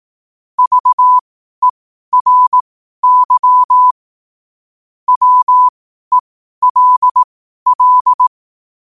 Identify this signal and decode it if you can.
Morse code, heard as a keyed tone, about 18 wpm, spaced as for 12 wpm, 1000 Hz. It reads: VERY WELL
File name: qotdfiles / 12wpm